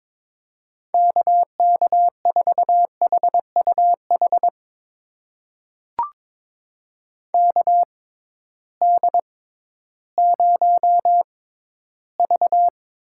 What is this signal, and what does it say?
Morse code, heard as a keyed tone, about 22 wpm, spaced as for 13 wpm, 700 Hz. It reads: KK4HUH KD0V
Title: USA Call Signs - 6 Characters - 6x Character Spacing - 22wpm